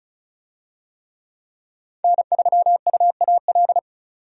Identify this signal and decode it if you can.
Morse code, heard as a keyed tone, about 34 wpm, 700 Hz. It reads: N3UAL
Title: USA Call Signs - 5 Characters - 35wpm